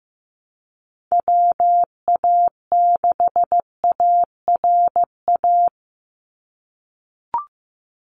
Morse code, heard as a keyed tone, about 15 wpm, 700 Hz. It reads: WA6ARA
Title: NA SOTA Chaser Callsigns 15wpm